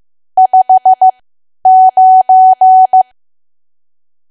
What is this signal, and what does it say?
Morse code, heard as a keyed tone, about 15 wpm, spaced as for 10 wpm, 750 Hz. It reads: 59
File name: cwa / W1AW-Level2 / 10WPM